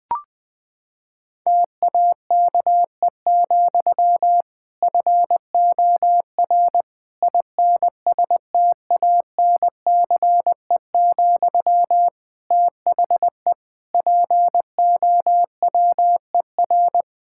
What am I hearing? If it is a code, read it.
Morse code, heard as a keyed tone, about 20 wpm, 700 Hz. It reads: TAKE, FOR INSTANCE, THE POWER
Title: Sentences from Top 1000 Words - Review Part 5 - 20wpm